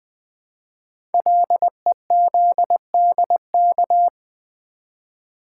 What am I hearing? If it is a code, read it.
Morse code, heard as a keyed tone, about 20 wpm, 700 Hz. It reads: LEZDK
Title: Code Groups - 5 Letters - 20wpm